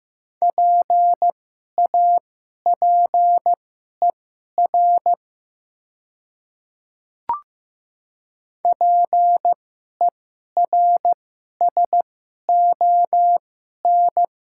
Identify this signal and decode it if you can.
Morse code, heard as a keyed tone, about 15 wpm, spaced as for 11 wpm, 700 Hz. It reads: PAPER PERSON
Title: 2x Character Spacing - Top 500 Words - 15wpm